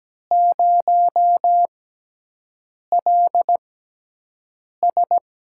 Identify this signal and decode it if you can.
Morse code, heard as a keyed tone, about 17 wpm, spaced as for 6 wpm, 700 Hz. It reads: 0LS
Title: USA Call Signs - 6x Character Spacing - 17wpm